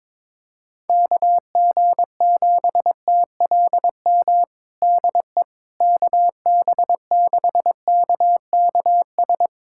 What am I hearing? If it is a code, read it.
Morse code, heard as a keyed tone, about 22 wpm, 700 Hz. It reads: KG7TLM DE KB6KKS
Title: USA Callsign DE Callsign - 6 Characters - 22wpm